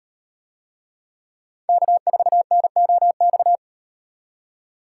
Morse code, heard as a keyed tone, about 38 wpm, 700 Hz. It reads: K4NOX